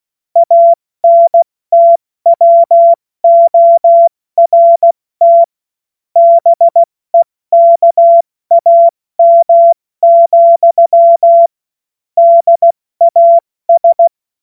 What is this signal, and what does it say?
Morse code, heard as a keyed tone, about 16 wpm, 680 Hz. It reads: ANTWORT BEKAM, DAS
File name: Morsekurs / Witze / Witz 25. Oktober 2022 70 F80